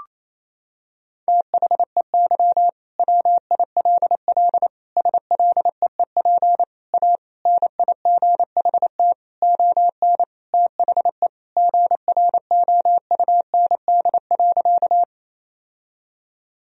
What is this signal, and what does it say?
Morse code, heard as a keyed tone, about 28 wpm, 700 Hz. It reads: THEY WILL SLEEP A NIGHT ON THE GROUND.
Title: Sentences from Top 1000 Words - Review Part 2 - 28wpm